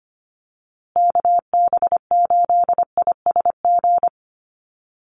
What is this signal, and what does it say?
Morse code, heard as a keyed tone, about 25 wpm, 700 Hz. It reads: KB8ISG
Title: USA Call Signs Repeated 2x 25wpm